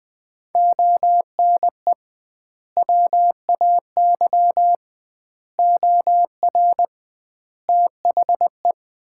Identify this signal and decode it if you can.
Morse code, heard as a keyed tone, about 20 wpm, 700 Hz. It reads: ONE WAY OR THE
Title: 2x Word-Spacing - Sets of 5 Words - 20wpm